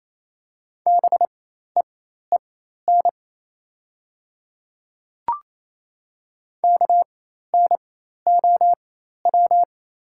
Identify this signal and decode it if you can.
Morse code, heard as a keyed tone, about 27 wpm, spaced as for 13 wpm, 700 Hz. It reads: BEEN KNOW
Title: 4x Character Spacing - Top 100 Words - 28wpm